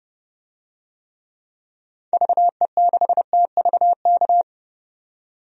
Morse code, heard as a keyed tone, about 31 wpm, 700 Hz. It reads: VE6TVK
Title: DXCC-100 Call Signs - 6 Characters - 30wpm